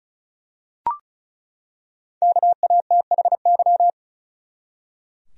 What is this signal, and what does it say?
Morse code, heard as a keyed tone, about 36 wpm, 700 Hz. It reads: KATHY
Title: USA Common Names - 35wpm